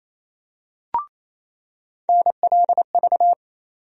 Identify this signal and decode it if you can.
Morse code, heard as a keyed tone, about 27 wpm, 700 Hz. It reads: NLV
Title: USA Call Sign Postfixes - 3 Letters 28wpm